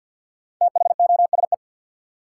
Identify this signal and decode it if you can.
Morse code, heard as a keyed tone, about 52 wpm, 700 Hz. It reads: THOSE
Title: Sentences from Top 300 Words - New Word Drill - 50wpm